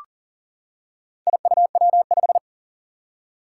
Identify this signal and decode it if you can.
Morse code, heard as a keyed tone, about 40 wpm, 700 Hz. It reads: IUW5
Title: Four Challenging Letter-Number Combinations 40wpm